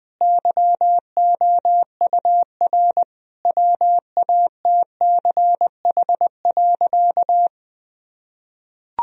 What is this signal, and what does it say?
Morse code, heard as a keyed tone, about 21 wpm, 700 Hz. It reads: YOUR WATCH.
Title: Sentences from Top 1000 Words - Review Part 1 - 20wpm